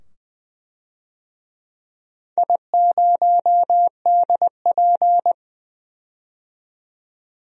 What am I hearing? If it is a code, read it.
Morse code, heard as a keyed tone, about 20 wpm, 700 Hz. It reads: I0DP
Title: DXCC-100 Call Signs - 4 Characters - 20wpm